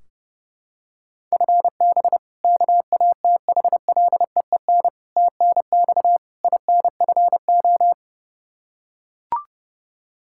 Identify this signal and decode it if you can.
Morse code, heard as a keyed tone, about 31 wpm, 700 Hz. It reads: FB KATHLEEN TNX INFO
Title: Practice QSO - Exchange 3 - More Info - 30wpm